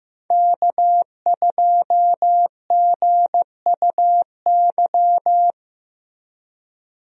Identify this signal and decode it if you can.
Morse code, heard as a keyed tone, about 15 wpm, 700 Hz. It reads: K2GUY
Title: USA Call Signs - 5 Characters - Repeated 3x - 15wpm